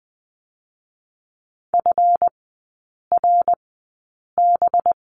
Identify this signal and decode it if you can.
Morse code, heard as a keyed tone, about 20 wpm, spaced as for 8 wpm, 700 Hz. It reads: FRB
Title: 184 Lesson 30e - Letter b - Three Characters 20wpm